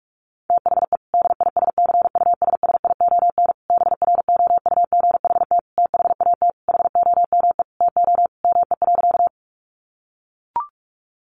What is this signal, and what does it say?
Morse code, heard as a keyed tone, about 44 wpm, 700 Hz. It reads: THE DISCUSSION BROUGHT THAT HOME TO ME.
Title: Sentences from Top 1100 Words 45wpm